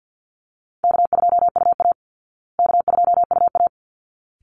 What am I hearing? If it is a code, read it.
Morse code, heard as a keyed tone, about 50 wpm, 700 Hz. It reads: K2UA K2UA
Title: Easy Call Signs Repeated 2x - 50wpm